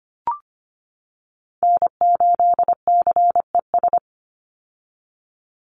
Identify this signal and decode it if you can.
Morse code, heard as a keyed tone, about 25 wpm, 700 Hz. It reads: N8CES
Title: USA Call Signs 25wpm